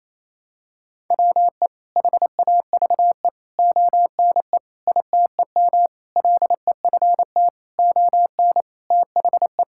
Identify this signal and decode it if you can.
Morse code, heard as a keyed tone, about 28 wpm, 700 Hz. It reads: WE HAVE ONE ITEM LEFT ON THE
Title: Sentences from Top 1000 Words - Review Part 6 - 28wpm